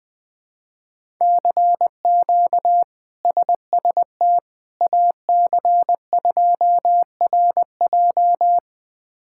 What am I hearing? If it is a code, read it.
Morse code, heard as a keyed tone, about 20 wpm, 700 Hz. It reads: CQ SST AC2RJ
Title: K1USN SST QSO Practice - 20wpm